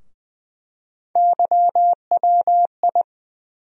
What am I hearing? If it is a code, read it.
Morse code, heard as a keyed tone, about 20 wpm, 700 Hz. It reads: YWI
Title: USA Call Sign Postfixes - 3 Letters 20wpm